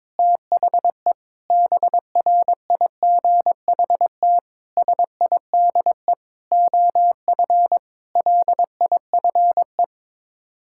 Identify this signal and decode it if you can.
Morse code, heard as a keyed tone, about 22 wpm, 700 Hz. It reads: THE BRIGHT SIDE OF LIFE
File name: practice / 22wpm / Sets of 8 Words - 22wpm